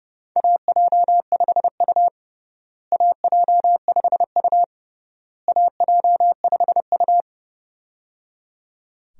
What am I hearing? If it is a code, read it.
Morse code, heard as a keyed tone, about 30 wpm, 700 Hz. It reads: AJ5U AJ5U AJ5U
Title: USA Call Signs - 4 Characters - Repeated 3x - 30wpm